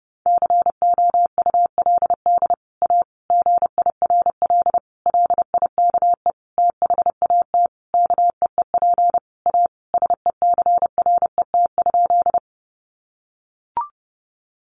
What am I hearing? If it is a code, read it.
Morse code, heard as a keyed tone, about 30 wpm, 700 Hz. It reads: COULD A GIRL LIKE THAT KEEP A SECRET?
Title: Sentences from Top 1200 Words 30wpm